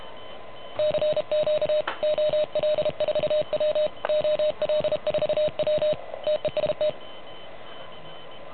Morse code, heard as a keyed tone, about 33 wpm, 610 Hz. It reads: CQ OL4W OL4W TEST
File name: truSDX / MWC29.8